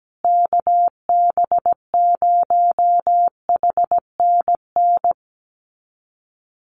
Morse code, heard as a keyed tone, about 17 wpm, 700 Hz. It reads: KB0HNN
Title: USA Call Signs Repeated 2x 17wpm